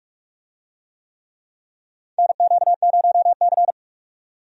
Morse code, heard as a keyed tone, about 46 wpm, 700 Hz. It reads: NQ0C
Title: USA Call Signs - 4 Characters - 45wpm